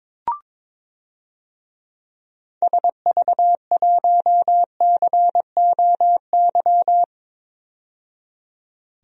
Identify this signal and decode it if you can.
Morse code, heard as a keyed tone, about 22 wpm, 700 Hz. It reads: SV1COY
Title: DXCC-100 Call Signs - 6 Characters - 22wpm